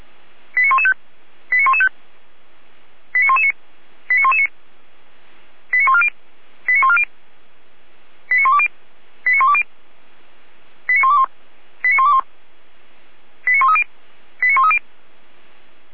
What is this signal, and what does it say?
Five-tone selective call (ZVEI), heard as quick runs of five tones, tones of about 80 ms. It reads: 89115 89115 89119 89119 89140 89140 89120 89120 89121 89121 89130 89130